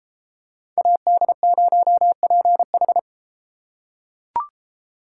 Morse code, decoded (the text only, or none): AD0PH